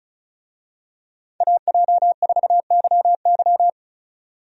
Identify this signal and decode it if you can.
Morse code, heard as a keyed tone, about 36 wpm, 700 Hz. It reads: AJ4YY
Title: USA Call Signs - 5 Characters - 35wpm